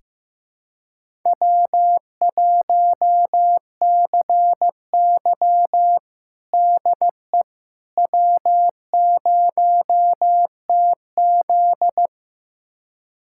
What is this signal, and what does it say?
Morse code, heard as a keyed tone, about 15 wpm, 700 Hz. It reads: W1CY DE W0TZ
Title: USA Callsign DE Callsign - 4 Characters - 15wpm